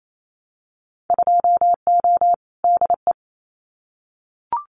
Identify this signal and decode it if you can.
Morse code, heard as a keyed tone, about 28 wpm, 700 Hz. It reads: 2O DE
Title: Winter Field Day Exchanges Encore 28wpm